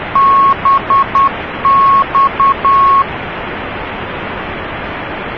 Morse code, heard as a keyed tone, about 10 wpm, 1100 Hz, with noise. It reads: BX